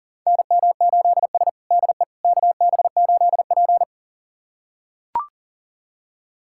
Morse code, heard as a keyed tone, about 37 wpm, 700 Hz. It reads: NM8S DE KB8P
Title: USA Callsign DE Callsign - 4 Characters - 40wpm